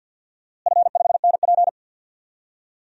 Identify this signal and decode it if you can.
Morse code, heard as a keyed tone, about 48 wpm, 700 Hz. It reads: F5NP
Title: DXCC-100 Call Signs - 4 Characters - Repeated 3x - 50wpm